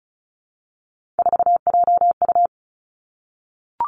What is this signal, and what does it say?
Morse code, heard as a keyed tone, about 36 wpm, 700 Hz. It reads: 4JU
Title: Three Challenging Letter-Number Combinations 35wpm